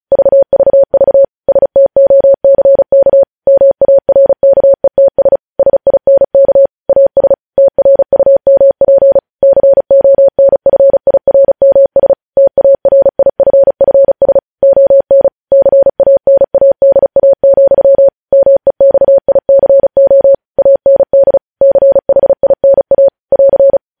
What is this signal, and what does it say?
Morse code, heard as a keyed tone, about 36 wpm, 550 Hz. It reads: VVV STOCK MARKETS SINK AS TRUMP CONFIRMS TARIFFS ON CANADA, MEXICO AND CHINA <AR>
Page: News Headlines in Morse Code at 35 WPM